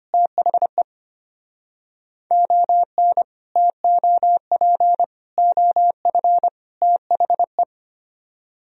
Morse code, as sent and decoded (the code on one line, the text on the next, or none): - .... / --- -. / - --- .--. / --- ..-. / - .... .
THE ON TOP OF THE